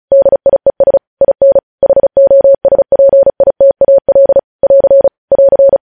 Morse code, heard as a keyed tone, about 36 wpm, 550 Hz. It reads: DIES IN HOSPITAL <AR> <AR>
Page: news headlines in morse code at 35 wpm